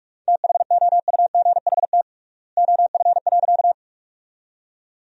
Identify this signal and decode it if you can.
Morse code, heard as a keyed tone, about 46 wpm, 700 Hz. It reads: THOUGHT OF.
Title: 3x Word-Spacing - Sentences from Top 300 Words - 45wpm